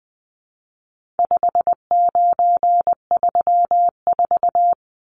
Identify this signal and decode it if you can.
Morse code, heard as a keyed tone, about 20 wpm, 700 Hz. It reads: 5934
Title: Numbers - 4-Digit 20wpm